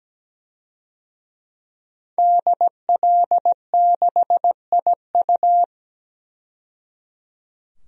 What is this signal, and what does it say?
Morse code, heard as a keyed tone, about 17 wpm, 700 Hz. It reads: DL6IU
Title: DXCC-100 Call Signs - 5 Characters - 17wpm